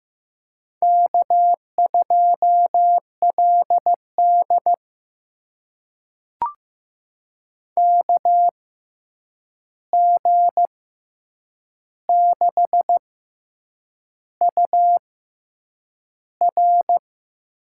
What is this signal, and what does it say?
Morse code, heard as a keyed tone, about 15 wpm, spaced as for 7 wpm, 700 Hz. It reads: K2LD KG6UR
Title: USA Call Signs - 6x Character Spacing - 15wpm